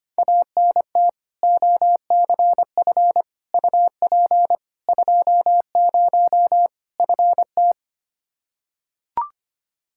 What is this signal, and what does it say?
Morse code, heard as a keyed tone, about 25 wpm, 700 Hz. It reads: ANT OCF UP 20 FT
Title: K1USN SST QSO Practice 25wpm